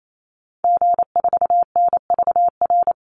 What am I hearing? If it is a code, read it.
Morse code, heard as a keyed tone, about 28 wpm, 700 Hz. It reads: G4NVR
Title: Call Signs Repeated 3x - 28wpm